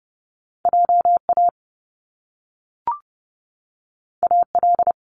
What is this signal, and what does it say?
Morse code, heard as a keyed tone, about 30 wpm, 700 Hz. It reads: JA AL
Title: Top Two Letter Combinations in Words 30wpm